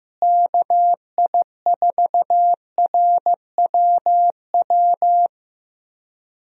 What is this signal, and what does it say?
Morse code, heard as a keyed tone, about 15 wpm, 700 Hz. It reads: KI4RWW
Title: USA Call Signs - 6 Characters - Repeated 2x - 15wpm